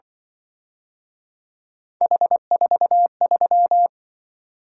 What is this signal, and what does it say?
Morse code, heard as a keyed tone, about 24 wpm, 700 Hz. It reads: H43
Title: Speed-Racing - Three Letter-Number Combinations - 20wpm